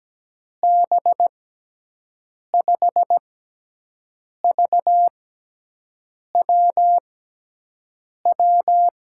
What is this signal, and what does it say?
Morse code, heard as a keyed tone, about 17 wpm, 700 Hz. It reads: B 5 V W W
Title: USA Call Signs - 6 Characters - 6x Character Spacing - 17wpm